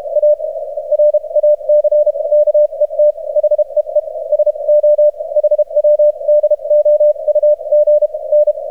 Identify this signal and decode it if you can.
Morse code, heard as a keyed tone, about 34 wpm, spaced as for 28 wpm, 600 Hz. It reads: A RACKET HEE SO HW DOUG N
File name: archived / CRSnet